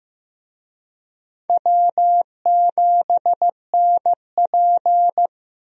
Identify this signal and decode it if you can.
Morse code, heard as a keyed tone, about 15 wpm, 700 Hz. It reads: W7NP
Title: USA Call Signs - 4 Characters - Repeated 3x - 15wpm